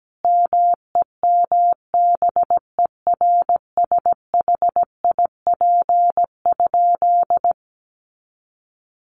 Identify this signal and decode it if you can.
Morse code, heard as a keyed tone, about 17 wpm, 700 Hz. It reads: MEMBERSHIP?
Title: Sentences from Top 1400 Words 17wpm